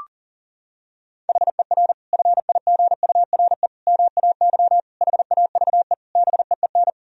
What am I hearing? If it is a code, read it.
Morse code, heard as a keyed tone, about 40 wpm, 700 Hz. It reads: HER FIGURE MAY HAVE BEEN
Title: Sentences from Top 1000 Words - Review Part 2 - 40wpm